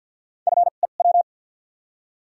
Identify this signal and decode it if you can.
Morse code, heard as a keyed tone, about 53 wpm, 700 Hz. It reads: F E W
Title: 2x Character Spacing - Top 300 Words - 50wpm